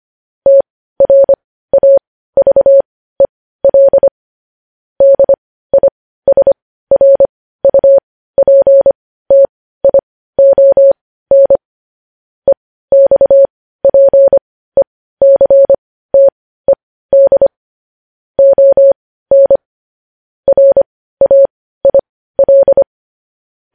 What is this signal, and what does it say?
Morse code, heard as a keyed tone, about 25 wpm, spaced as for 15 wpm, 550 Hz. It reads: TRAVEL DISRUPTION EXPECTED ON RAIL
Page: news-headlines-in-morse-code-at-15-wpm